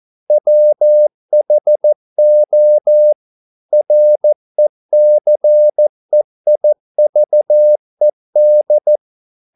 Morse code, heard as a keyed tone, about 14 wpm, 600 Hz. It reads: WHO RECEIVED
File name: mp3 files / a dogs mistake - 14wpm